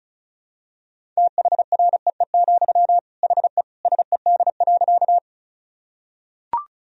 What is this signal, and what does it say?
Morse code, heard as a keyed tone, about 34 wpm, 700 Hz. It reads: THREE, HE SED.
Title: Sentences from Top 200 Words and Abbreviations 35wpm